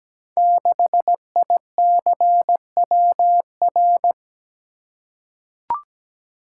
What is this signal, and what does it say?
Morse code, heard as a keyed tone, about 17 wpm, 700 Hz. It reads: 6ICWR
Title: Code Groups - 5 Letter:Numbers - 17wpm